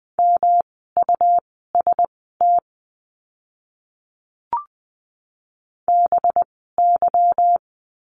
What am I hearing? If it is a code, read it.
Morse code, heard as a keyed tone, about 20 wpm, spaced as for 15 wpm, 700 Hz. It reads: MUST BY
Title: 237 Lesson 38f - Letter q - Words 20wpm